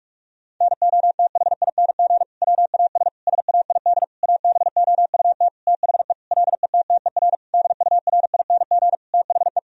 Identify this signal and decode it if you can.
Morse code, heard as a keyed tone, about 46 wpm, 700 Hz. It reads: NOTHING WAS SAID ABOUT THE LETTER DURING THE
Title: Sentences from Top 1000 Words - Review Part 2 - 45wpm